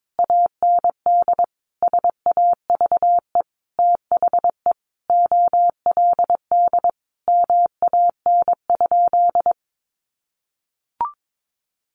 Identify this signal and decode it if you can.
Morse code, heard as a keyed tone, about 22 wpm, 700 Hz. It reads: AND SAVE THE OLD MAN?
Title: Sentences from Top 1100 Words 22wpm